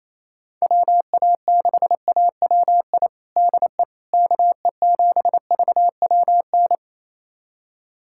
Morse code, heard as a keyed tone, about 27 wpm, 700 Hz. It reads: WA6AWI DE KE7VWN
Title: USA Callsign DE Callsign - 6 Characters - 28wpm